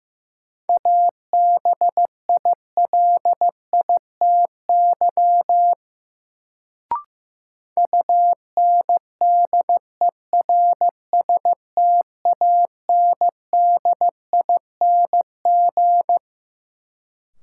Morse code, heard as a keyed tone, about 15 wpm, 700 Hz. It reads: ABILITY UNDERSTANDING
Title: Sentences from Top 1000 Words - Review Part 3 - New Word Drill - 15wpm